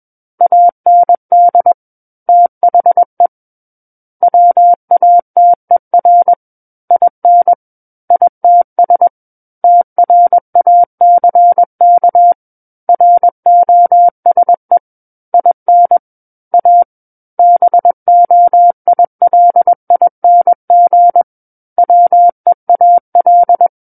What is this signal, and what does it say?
Morse code, heard as a keyed tone, about 21 wpm, 700 Hz. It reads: AND THE WATER IN ITS TRACK ROSE IN A BOILING WEAL
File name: Book One / 12-Chapter 12 - 21 WPM